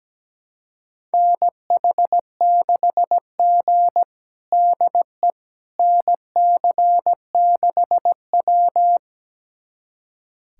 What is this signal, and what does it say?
Morse code, heard as a keyed tone, about 17 wpm, 700 Hz. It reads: NH6G DE NC6W
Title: USA Callsign DE Callsign - 4 Characters - 17wpm